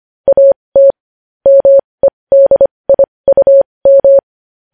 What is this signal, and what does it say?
Morse code, heard as a keyed tone, about 25 wpm, spaced as for 20 wpm, 550 Hz. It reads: AT MEDIUM
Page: news-headlines-in-morse-code-at-20-wpm